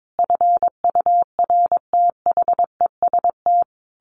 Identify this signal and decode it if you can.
Morse code, heard as a keyed tone, about 22 wpm, 700 Hz. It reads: FURTHEST
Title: A Sound Of Thunder 22wpm